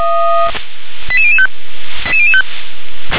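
Five-tone selective call (ZVEI), heard as quick runs of five tones, tones of about 80 ms. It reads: 88005
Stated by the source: ZVEI five-tone selective call 88104 received via SDR.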